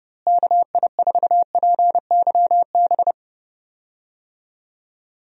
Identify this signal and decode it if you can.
Morse code, heard as a keyed tone, about 30 wpm, 700 Hz. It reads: KI4PYB
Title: USA Call Signs - 6 Characters - Repeated 3x - 30wpm